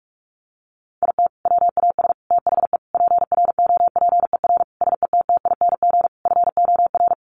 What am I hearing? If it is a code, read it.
Morse code, heard as a keyed tone, about 44 wpm, 700 Hz. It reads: IT WAS THE PROPER SETTING FOR
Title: Sentences from Top 1300 Words 45wpm